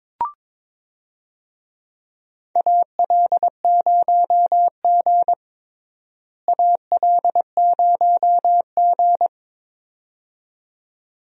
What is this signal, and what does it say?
Morse code, heard as a keyed tone, about 22 wpm, 700 Hz. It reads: AL0G AL0G
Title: USA Call Signs - 4 Characters - Repeated 2x - 22wpm